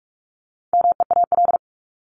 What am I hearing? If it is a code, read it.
Morse code, heard as a keyed tone, about 44 wpm, 700 Hz. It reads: MEAL
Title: Top 1751-2000 Words 45wpm